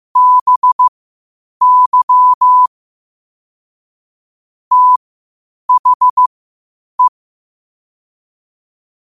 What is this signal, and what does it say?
Morse code, heard as a keyed tone, about 15 wpm, spaced as for 8 wpm, 1000 Hz. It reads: BY THE
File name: qotdfiles / 8wpm